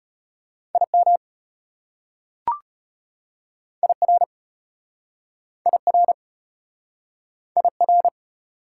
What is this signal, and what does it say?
Morse code, heard as a keyed tone, about 34 wpm, 690 Hz. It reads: IM IR IR IR